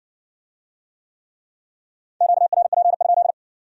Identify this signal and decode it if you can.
Morse code, heard as a keyed tone, about 60 wpm, 700 Hz. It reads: QRP?